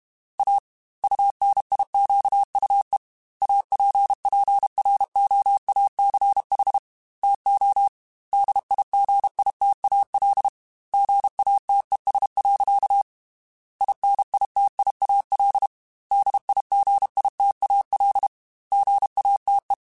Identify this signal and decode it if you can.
Morse code, heard as a keyed tone, about 32 wpm, 800 Hz. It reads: A UNIQUE APPROACH TO DIGITAL GATES. INITIAL DIGITAL GATE
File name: Technical Subjects / Technical Subjects-mp3s / Gates Intro @30